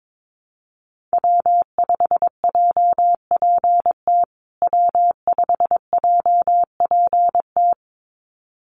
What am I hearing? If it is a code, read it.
Morse code, heard as a keyed tone, about 22 wpm, 700 Hz. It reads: W5JPT W5JPT
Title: USA Call Signs Repeated 2x 22wpm